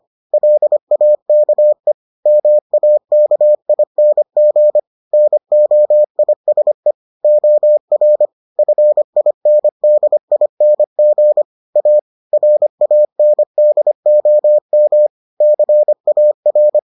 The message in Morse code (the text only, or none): LAKE MAKING NOISE OR FINDING A RANDOM CAR